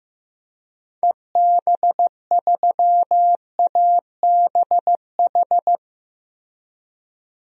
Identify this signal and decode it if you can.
Morse code, heard as a keyed tone, about 15 wpm, 700 Hz. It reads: EB3ABH
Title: DXCC-100 Call Signs - 6 Characters - Repeated 3x - 15wpm